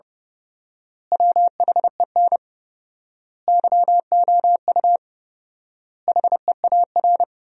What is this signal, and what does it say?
Morse code, heard as a keyed tone, about 30 wpm, 700 Hz. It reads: WHEN YOU HEAR
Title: 4x Word-Spacing - Sentences from Top 300 Words - 30wpm